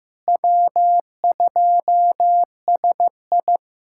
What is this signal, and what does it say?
Morse code, heard as a keyed tone, about 15 wpm, 700 Hz. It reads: W2SI
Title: USA Callsign DE Callsign - 4 Characters - 15wpm